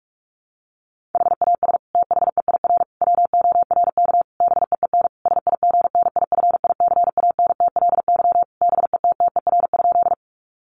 Morse code, heard as a keyed tone, about 44 wpm, 700 Hz. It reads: HAS THEIR WORK BEEN SIGNIFICANTLY BETTER?
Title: Sentences from Top 1500 Words 45wpm